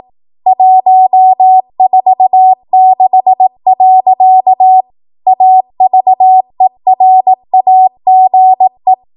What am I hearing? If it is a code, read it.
Morse code, heard as a keyed tone, about 18 wpm, 750 Hz. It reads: 146. AVERAGE
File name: Archive / 18 WPM / 18WPM